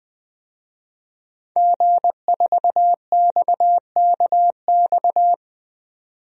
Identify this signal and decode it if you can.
Morse code, heard as a keyed tone, about 20 wpm, 700 Hz. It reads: G4XKX
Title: DXCC-100 Call Signs - 5 Characters - 20wpm